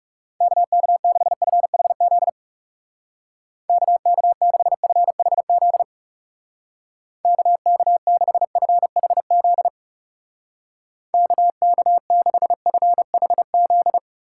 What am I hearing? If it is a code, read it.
Morse code, heard as a keyed tone, about 40 wpm, 700 Hz. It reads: KK6FHZ KK6FHZ KK6FHZ KK6FHZ